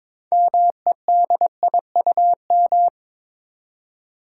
Morse code, heard as a keyed tone, about 22 wpm, 700 Hz. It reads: MEDIUM
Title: Six Letter Words - 22wpm